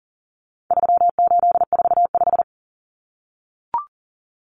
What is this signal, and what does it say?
Morse code, heard as a keyed tone, about 40 wpm, 700 Hz. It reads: 3845
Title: Common Cut Numbers - 4-Digit 40wpm